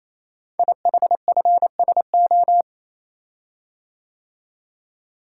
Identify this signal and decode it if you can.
Morse code, heard as a keyed tone, about 27 wpm, 700 Hz. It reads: IHFSO
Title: Code Groups - 5 Letters - 28wpm